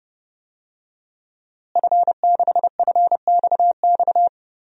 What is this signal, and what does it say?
Morse code, heard as a keyed tone, about 30 wpm, 700 Hz. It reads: F6FXX
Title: DXCC-100 Call Signs - 5 Characters - Repeated 3x - 30wpm